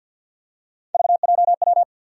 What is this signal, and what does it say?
Morse code, heard as a keyed tone, about 52 wpm, 700 Hz. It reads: VJW
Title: USA Call Sign Postfixes - 3 Letters 50wpm